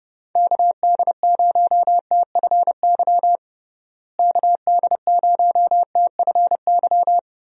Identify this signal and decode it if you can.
Morse code, heard as a keyed tone, about 32 wpm, 700 Hz. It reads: KD0TFY KD0TFY
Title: USA Call Signs - 6 Characters - Repeated 2x - 30wpm